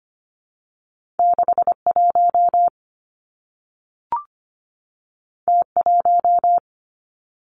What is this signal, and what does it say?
Morse code, heard as a keyed tone, about 25 wpm, 700 Hz. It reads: 61 T1
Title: Two Challenging Letter-Number Combinations 25wpm